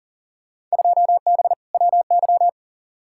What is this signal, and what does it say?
Morse code, heard as a keyed tone, about 40 wpm, 700 Hz. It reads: 2B WY
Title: Field Day Exchanges Encore - 40wpm